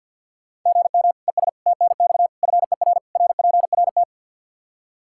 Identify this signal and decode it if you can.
Morse code, heard as a keyed tone, about 46 wpm, 700 Hz. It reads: GM ES TNX FER RPRT
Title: Practice QSO - Exchange 1 - Beginning - 50wpm